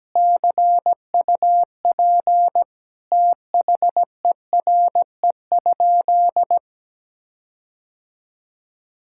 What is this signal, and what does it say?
Morse code, heard as a keyed tone, about 17 wpm, 700 Hz. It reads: CUP THERE?
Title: Sentences from Top 1600 Words 17wpm